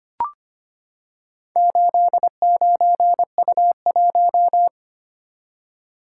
Morse code, heard as a keyed tone, about 26 wpm, 700 Hz. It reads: 89U1
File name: practice / 25wpm / Four Letter-Number Combinations - 25wpm